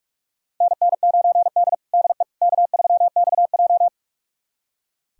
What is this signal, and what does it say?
Morse code, heard as a keyed tone, about 44 wpm, 690 Hz. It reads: NN9D DE K3XJ
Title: USA Callsign DE Callsign - 4 Characters - 45wpm